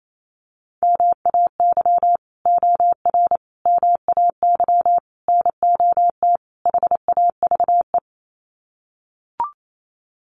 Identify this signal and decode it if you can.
Morse code, heard as a keyed tone, about 28 wpm, 700 Hz. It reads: MAY OR MAY NOT HAVE
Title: Sets of 5 Words Encore2 28wpm